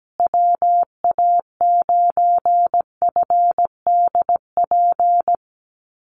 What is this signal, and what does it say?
Morse code, heard as a keyed tone, about 17 wpm, 700 Hz. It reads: WA9FDP
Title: USA Call Signs Repeated 2x 17wpm